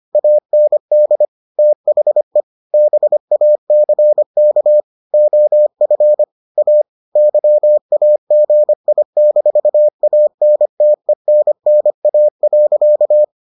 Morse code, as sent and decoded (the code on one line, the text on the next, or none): .- -. -.. / - .... / -... .- -.-. -.- / --- ..-. / .- / -.-- .- --. .. -....- .- -. - . -. -. .- .-.-.-
AND THE BACK OF A YAGI-ANTENNA.